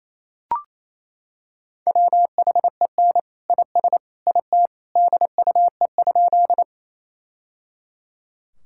Morse code, heard as a keyed tone, about 28 wpm, 700 Hz. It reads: WHEN IS IT DUE?
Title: Sentences from Top 1000 Words - Review Part 6 - 28wpm